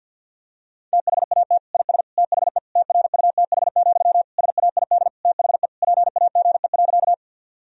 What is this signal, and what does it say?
Morse code, heard as a keyed tone, about 49 wpm, 700 Hz. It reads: THAT IS THE TRUTH, SAID THE PAGE.
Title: Sentences from Top 1000 Words - Review Part 1 - 50wpm